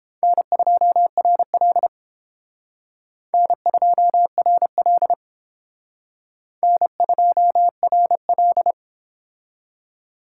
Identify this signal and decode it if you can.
Morse code, heard as a keyed tone, about 30 wpm, 700 Hz. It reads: N2RL N2RL N2RL